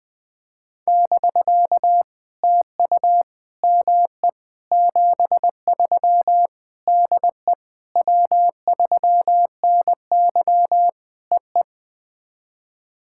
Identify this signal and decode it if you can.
Morse code, heard as a keyed tone, about 20 wpm, 700 Hz. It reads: <BK> TU ME 73 DE W3NY EE
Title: POTA QSO Practice - 20wpm